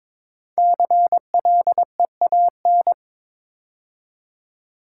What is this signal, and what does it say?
Morse code, heard as a keyed tone, about 22 wpm, 700 Hz. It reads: CLEAN
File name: 5 Letter Words - First 3 Letters Given - 22wpm